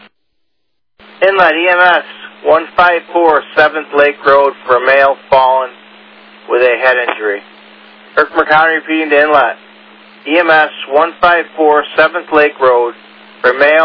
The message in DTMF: none

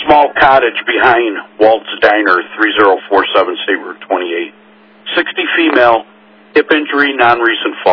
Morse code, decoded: none

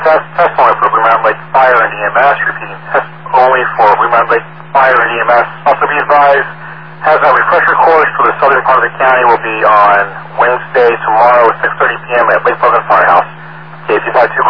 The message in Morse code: none

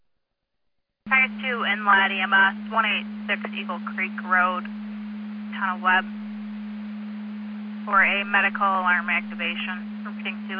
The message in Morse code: none

